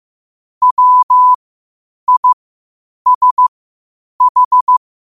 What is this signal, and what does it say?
Morse code, heard as a keyed tone, about 15 wpm, spaced as for 8 wpm, 1000 Hz. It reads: WISH
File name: qotdfiles / 8wpm